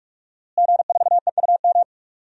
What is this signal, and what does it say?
Morse code, heard as a keyed tone, about 44 wpm, 700 Hz. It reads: G4EUM
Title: DXCC-100 Call Signs - 5 Characters - Repeated 3x - 45wpm